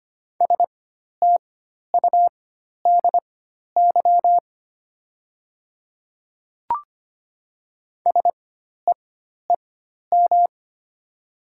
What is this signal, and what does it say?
Morse code, heard as a keyed tone, about 25 wpm, spaced as for 12 wpm, 700 Hz. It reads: STUDY SEEM
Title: 4x Character Spacing - Top 300 Words - 25wpm